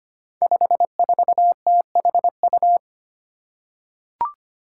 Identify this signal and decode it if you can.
Morse code, heard as a keyed tone, about 25 wpm, 700 Hz. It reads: 54THU
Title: Five Challenging Letter-Number Combinations 25wpm